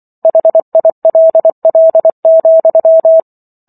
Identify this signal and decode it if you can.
Morse code, heard as a keyed tone, about 24 wpm, 650 Hz. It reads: HILL,